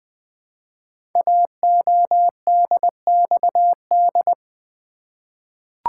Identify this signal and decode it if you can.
Morse code, heard as a keyed tone, about 20 wpm, 700 Hz. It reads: AODXD
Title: Code Groups - 5 Letters - 20wpm